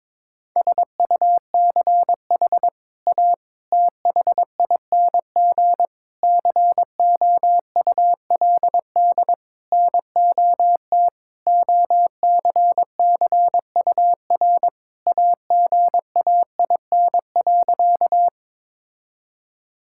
Morse code, heard as a keyed tone, about 22 wpm, 700 Hz. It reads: SUCH A THING COULD NOT OCCUR AGAIN.
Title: Sentences from Top 1000 Words - Review Part 6 - 22wpm